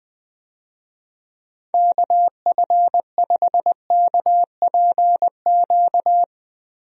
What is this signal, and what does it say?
Morse code, heard as a keyed tone, about 20 wpm, 700 Hz. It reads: KF5KPQ
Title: USA Call Signs - 6 Characters - Repeated 2x - 20wpm